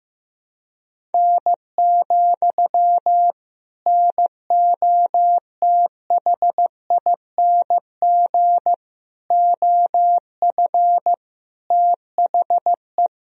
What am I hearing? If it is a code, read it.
Morse code, heard as a keyed tone, about 15 wpm, 700 Hz. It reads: N, NOTHING OF THE
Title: Sentences from Top 500 Words and Abbreviations 15wpm